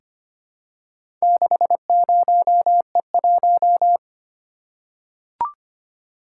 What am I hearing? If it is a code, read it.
Morse code, heard as a keyed tone, about 25 wpm, 700 Hz. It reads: 60E1
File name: practice / 25wpm / Four Letter-Number Combinations - 25wpm